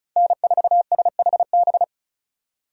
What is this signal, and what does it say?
Morse code, heard as a keyed tone, about 36 wpm, 700 Hz. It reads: N4SHB
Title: USA Call Signs - 2x Character Spacing - 35wpm